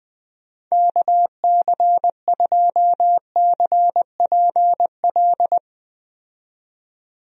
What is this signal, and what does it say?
Morse code, heard as a keyed tone, about 20 wpm, 700 Hz. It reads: KC2CPL
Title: USA Call Signs - 6 Characters - 20wpm